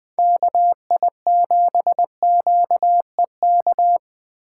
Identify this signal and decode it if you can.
Morse code, heard as a keyed tone, about 21 wpm, 700 Hz. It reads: KI7QEK
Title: 2023 SOTA Chaser Call Signs - 20wpm